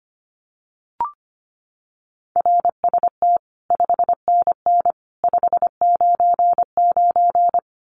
Morse code, heard as a keyed tone, about 25 wpm, 700 Hz. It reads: RST 5NN 599
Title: Common QSO Phrases 25wpm